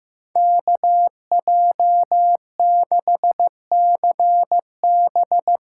Morse code, heard as a keyed tone, about 15 wpm, 700 Hz. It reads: KJ6CB
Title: USA Call Signs - 8x Character Spacing - 15wpm